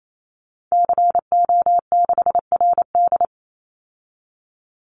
Morse code, heard as a keyed tone, about 28 wpm, 700 Hz. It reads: CO6RD
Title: Call Signs Repeated 3x - 28wpm